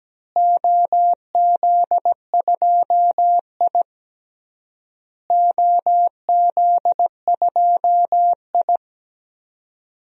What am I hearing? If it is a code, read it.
Morse code, heard as a keyed tone, about 17 wpm, 700 Hz. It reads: OZ2I OZ2I
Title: DXCC-100 Call Signs - 4 Characters - Repeated 3x - 17wpm